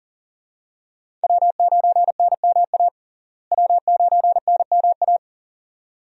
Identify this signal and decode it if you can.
Morse code, heard as a keyed tone, about 40 wpm, 700 Hz. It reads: W9NMA W9NMA
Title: USA Call Signs - 5 Characters - Repeated 2x - 40wpm